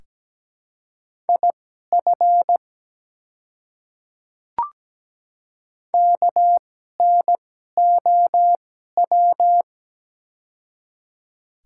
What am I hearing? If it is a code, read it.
Morse code, heard as a keyed tone, about 17 wpm, spaced as for 12 wpm, 700 Hz. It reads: IF KNOW
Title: 2x Character Spacing - Top 100 Words - 17wpm